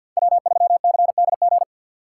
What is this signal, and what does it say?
Morse code, heard as a keyed tone, about 51 wpm, 700 Hz. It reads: W3CDG
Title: USA Call Signs - 3x Character Spacing - 50wpm